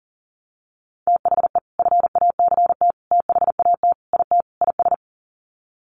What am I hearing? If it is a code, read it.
Morse code, heard as a keyed tone, about 40 wpm, 700 Hz. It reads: THE FACT THAT IT IS
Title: Sets of 5 Words Encore2 40wpm